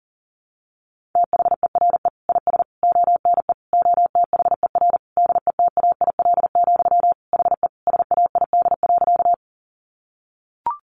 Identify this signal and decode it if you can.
Morse code, heard as a keyed tone, about 40 wpm, 700 Hz. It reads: THERE IS ONE OTHER DETAIL, HE SAID.
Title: Sentences from Top 1500 Words 40wpm